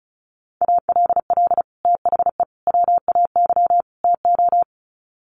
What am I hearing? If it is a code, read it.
Morse code, heard as a keyed tone, about 34 wpm, 700 Hz. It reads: ALL THE WAY TO
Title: Sets of 4 Words 35wpm